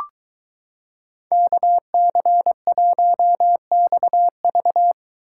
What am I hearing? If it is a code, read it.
Morse code, heard as a keyed tone, about 23 wpm, 700 Hz. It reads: KC1XV